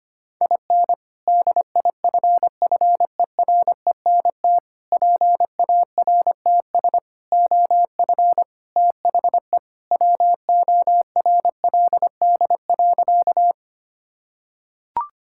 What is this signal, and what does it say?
Morse code, heard as a keyed tone, about 25 wpm, 700 Hz. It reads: IN DIFFERENT PARTS OF THE WORLD.
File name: practice / 25wpm / Sentences with 3-Word Phrases - 25wpm